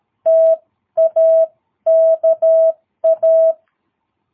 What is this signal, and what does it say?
Morse code, heard as a keyed tone, about 13 wpm, spaced as for 11 wpm, 660 Hz. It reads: TAKA
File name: jh8cbh / 2022cw